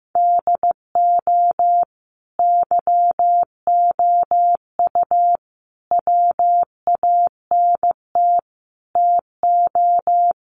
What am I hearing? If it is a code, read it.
Morse code, heard as a keyed tone, about 15 wpm, 700 Hz. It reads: DO YOU WANT TO
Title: Sets of 4 Words 15wpm